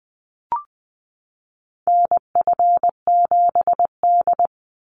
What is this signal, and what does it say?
Morse code, heard as a keyed tone, about 20 wpm, 700 Hz. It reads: NF7D
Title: Call Signs - 20wpm